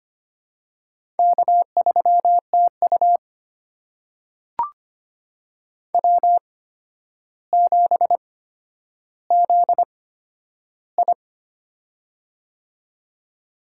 Morse code, heard as a keyed tone, about 25 wpm, spaced as for 11 wpm, 700 Hz. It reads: K3TU W7ZI
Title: USA Call Signs - 4 Characters - 8x Character Spacing - 25wpm